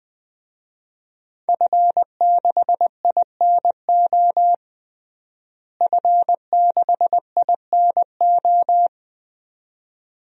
Morse code, heard as a keyed tone, about 20 wpm, 700 Hz. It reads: F6INO F6INO
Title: DXCC-100 Call Signs - 5 Characters - Repeated 2x - 20wpm